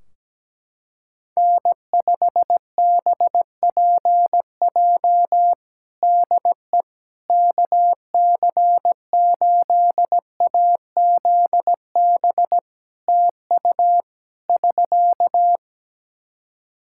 Morse code, heard as a keyed tone, about 17 wpm, 700 Hz. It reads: N5BPJ DE KC8AZB TU <SK>
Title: K1USN SST QSO Practice 17wpm